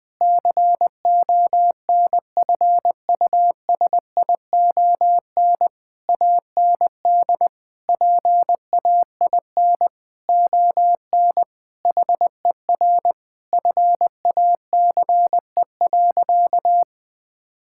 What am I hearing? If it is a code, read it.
Morse code, heard as a keyed tone, about 20 wpm, 700 Hz. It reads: CONFUSION AND PAIN ON HER FACE.
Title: Sentences from Top 1600 Words 20wpm